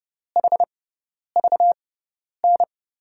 Morse code, heard as a keyed tone, about 30 wpm, 700 Hz. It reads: H V N
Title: USA Call Signs - 5 Characters - 6x Character Spacing - 30wpm